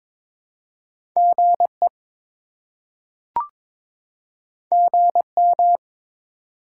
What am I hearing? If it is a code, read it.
Morse code, heard as a keyed tone, about 21 wpm, 700 Hz. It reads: GE GM
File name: practice / 22wpm / Top 20 QSO Elements - 22wpm